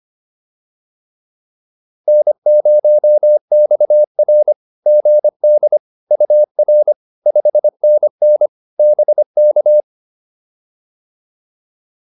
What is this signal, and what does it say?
Morse code, heard as a keyed tone, about 25 wpm, 590 Hz. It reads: N0XR GD UR 5NN BK